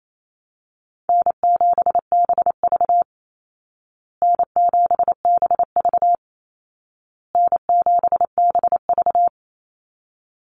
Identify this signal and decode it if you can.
Morse code, heard as a keyed tone, about 28 wpm, 700 Hz. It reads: N7BV N7BV N7BV
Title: Call Signs Repeated 2x - 28wpm